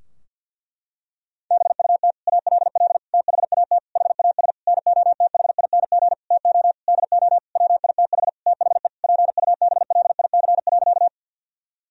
Morse code, heard as a keyed tone, about 51 wpm, 700 Hz. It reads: BUT ALL THAT HAS NOTHING TO DO WITH THE PUBLIC.
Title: Sentences from Top 1000 Words - Review Part 3 - 50wpm